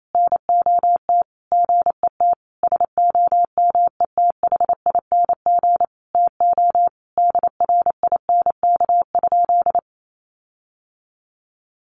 Morse code, heard as a keyed tone, about 28 wpm, 700 Hz. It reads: NOT GET SOMETHING TO DRINK?
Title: Sentences from Top 1500 Words 28wpm